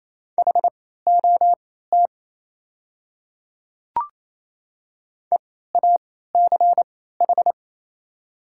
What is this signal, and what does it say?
Morse code, heard as a keyed tone, about 28 wpm, spaced as for 16 wpm, 700 Hz. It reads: HOT EACH
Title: 3x Character Spacing - Top 100 Words - 28wpm